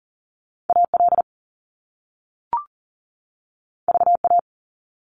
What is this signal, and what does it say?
Morse code, heard as a keyed tone, about 38 wpm, 700 Hz. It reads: AL VA
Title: US States 40wpm